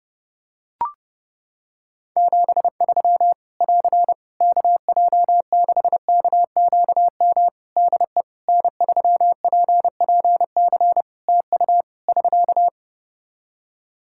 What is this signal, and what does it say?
Morse code, heard as a keyed tone, about 30 wpm, 700 Hz. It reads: 73 <AR> KJ6KQM DE N3PPC TU <SK>
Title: K1USN SST QSO Practice 30wpm